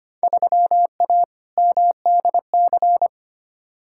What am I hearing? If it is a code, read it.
Morse code, heard as a keyed tone, about 25 wpm, 700 Hz. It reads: 3A MDC
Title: Field Day Exchanges - 25wpm